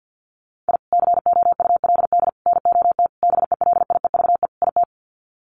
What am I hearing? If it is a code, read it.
Morse code, heard as a keyed tone, about 48 wpm, 700 Hz. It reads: I COULD NOT BELIEVE IT
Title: Sets of 5 Words Encore 50wpm